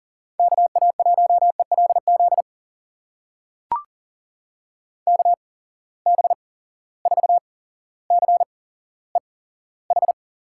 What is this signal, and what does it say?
Morse code, heard as a keyed tone, about 43 wpm, spaced as for 17 wpm, 700 Hz. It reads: KA1ELZ KB4CEH
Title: USA Call Signs - 8x Character Spacing - 40wpm